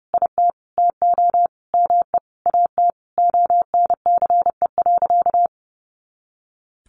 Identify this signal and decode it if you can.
Morse code, heard as a keyed tone, about 30 wpm, 700 Hz. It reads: IT TO ME AT ONCE.
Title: Sentences from Top 500 Words 30wpm